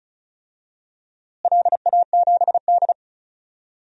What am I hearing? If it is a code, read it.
Morse code, heard as a keyed tone, about 36 wpm, 690 Hz. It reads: LA7D
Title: DXCC-100 Call Signs - 4 Characters - Repeated 2x - 35wpm